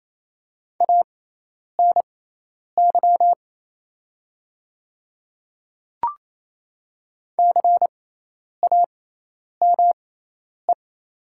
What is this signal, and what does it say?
Morse code, heard as a keyed tone, about 28 wpm, spaced as for 10 wpm, 700 Hz. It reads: ANY CAME
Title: 6x Character Spacing - Top 200 Words - 28wpm